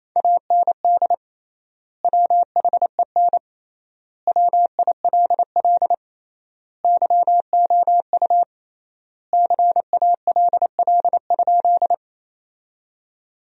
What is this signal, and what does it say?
Morse code, heard as a keyed tone, about 28 wpm, 700 Hz. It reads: AND WHEN WILL YOU CALL?
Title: 3x Word-Spacing - Sentences from Top 100 Words - 28wpm